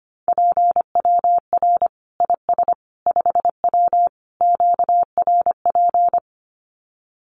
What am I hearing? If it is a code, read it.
Morse code, heard as a keyed tone, about 25 wpm, 700 Hz. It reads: PWR IS 5W QRP
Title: Common QSO Phrases 25wpm